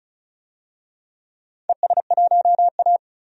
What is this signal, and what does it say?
Morse code, heard as a keyed tone, about 36 wpm, 700 Hz. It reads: ES1A
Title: DXCC-100 Call Signs - 4 Characters - 35wpm